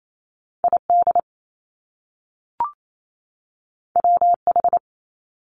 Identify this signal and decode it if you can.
Morse code, heard as a keyed tone, about 28 wpm, 700 Hz. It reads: ID WH
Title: Top Two Letter Combinations in Words 28wpm